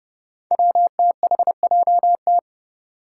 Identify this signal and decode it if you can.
Morse code, heard as a keyed tone, about 30 wpm, 700 Hz. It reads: WTHJT
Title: Five Challenging Letter-Number Combinations 30wpm